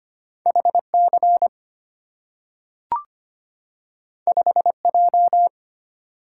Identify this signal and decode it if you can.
Morse code, heard as a keyed tone, about 25 wpm, 700 Hz. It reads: HC 5J
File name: Two Letter-Number Combinations - 25wpm